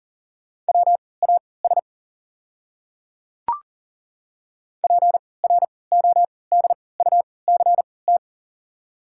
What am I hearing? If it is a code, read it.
Morse code, heard as a keyed tone, about 40 wpm, spaced as for 23 wpm, 690 Hz. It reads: WAS PRODUCT
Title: 3x Character Spacing - Top 500 Words - 40wpm